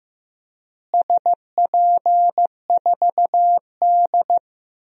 Morse code, heard as a keyed tone, about 15 wpm, 700 Hz. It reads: SP4D
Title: DXCC-100 Call Signs - 4 Characters - Repeated 3x - 15wpm